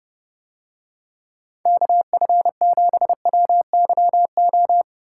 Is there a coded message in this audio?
Yes: Morse code from 1.7 to 4.8 s, heard as a keyed tone, about 31 wpm, 700 Hz. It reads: KF7WYO